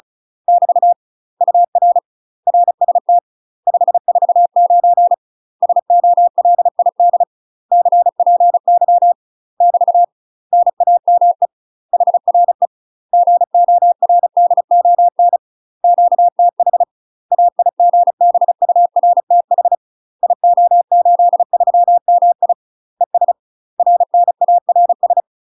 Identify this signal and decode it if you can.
Morse code, heard as a keyed tone, about 36 wpm, 700 Hz. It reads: <BT> UR RST 549 SOLID CPY <BT> NAME HRE GORDON QTH AIGBURTH IO83MI ES RNARS
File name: UK MASTER QSOs TRAINING / 35WPM / UK MASTER QSOs 0018